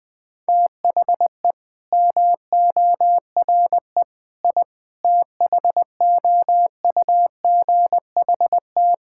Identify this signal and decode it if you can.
Morse code, heard as a keyed tone, about 20 wpm, 700 Hz. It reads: THE MORE I THOUGHT
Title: Sentences with 9-Word Phrases - 20wpm